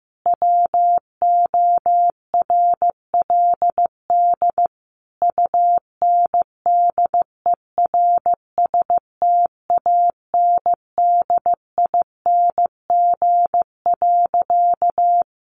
Morse code, heard as a keyed tone, about 15 wpm, 700 Hz. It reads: WORLD UNDERSTANDING.
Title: Sentences from Top 1400 Words 15wpm